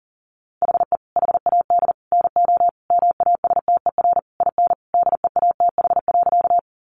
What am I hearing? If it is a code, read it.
Morse code, heard as a keyed tone, about 40 wpm, 700 Hz. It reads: HE HAD NO MASTER IN DEATH.
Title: Sentences from Top 1300 Words 40wpm